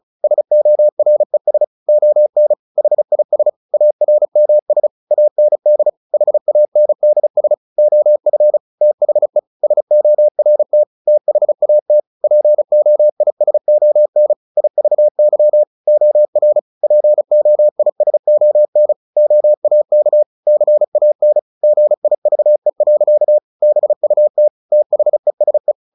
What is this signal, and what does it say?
Morse code, heard as a keyed tone, about 34 wpm, 600 Hz. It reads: SORES ON HIS ARMS AND HANDS OF THE SORT THAT POISON IVY OR POISON OAK CAN GIVE. BUT THESE
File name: the old man and the sea / the old man and the sea 1 35wpm